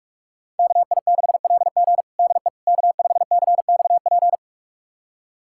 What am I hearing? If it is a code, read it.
Morse code, heard as a keyed tone, about 44 wpm, 700 Hz. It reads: KI6LG DE K5CXP
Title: USA Callsign DE Callsign - 5 Characters - 45wpm